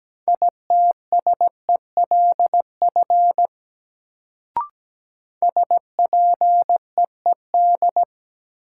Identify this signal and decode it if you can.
Morse code, heard as a keyed tone, about 17 wpm, 700 Hz. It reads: ITSELF SPEED
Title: Sentences from Top 800 Words - New Word Drill - 17wpm